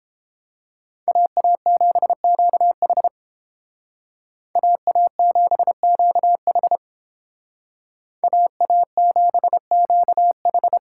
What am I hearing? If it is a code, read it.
Morse code, heard as a keyed tone, about 30 wpm, 700 Hz. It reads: AA7QH AA7QH AA7QH